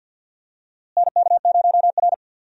Morse code, heard as a keyed tone, about 47 wpm, 700 Hz. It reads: NK0R
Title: USA Call Signs - 4 Characters - Repeated 2x - 50wpm